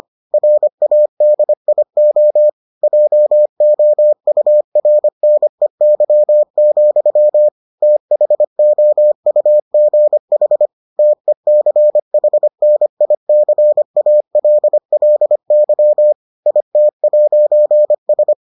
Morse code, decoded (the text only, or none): RADIO JOURNEY, THOUGH TECHNICALLY IT'S